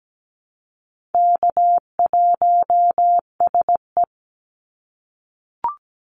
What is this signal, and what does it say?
Morse code, heard as a keyed tone, about 17 wpm, 700 Hz. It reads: K1SE
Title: Easy Call Signs - 17wpm